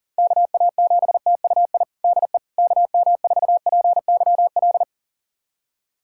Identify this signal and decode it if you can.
Morse code, heard as a keyed tone, about 40 wpm, 700 Hz. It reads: KA7TUI DE KM4PYL
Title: USA Callsign DE Callsign 40wpm